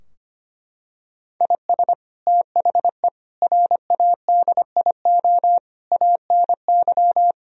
Morse code, heard as a keyed tone, about 25 wpm, 700 Hz. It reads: IS THE RADIO ANY
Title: Sentences from Top 1000 Words - Review Part 3 - 25wpm